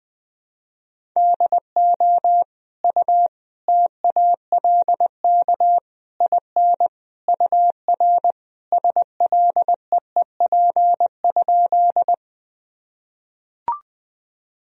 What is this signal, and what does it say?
Morse code, heard as a keyed tone, about 20 wpm, 700 Hz. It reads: DO U TALK IN UR SLEEP?
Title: Sentences from Top 500 Words and Abbreviations 20wpm